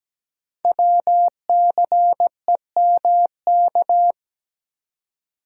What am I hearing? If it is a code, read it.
Morse code, heard as a keyed tone, about 17 wpm, 700 Hz. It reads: WCEMK
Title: Code Groups - 5 Letters - 17wpm